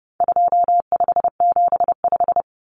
Morse code, heard as a keyed tone, about 30 wpm, 700 Hz. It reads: 2575
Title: Numbers - 4-Digit 30wpm